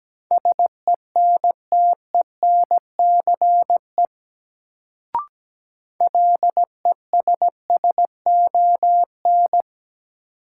Morse code, heard as a keyed tone, about 17 wpm, 700 Hz. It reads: SENTENCE LESSON
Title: Sentences from Top 1200 Words - New Word Drill Alt - 17wpm